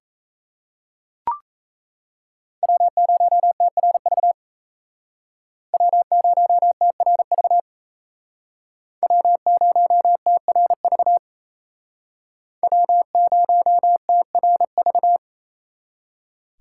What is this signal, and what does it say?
Morse code, heard as a keyed tone, about 37 wpm, 700 Hz. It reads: W0TRV W0TRV W0TRV W0TRV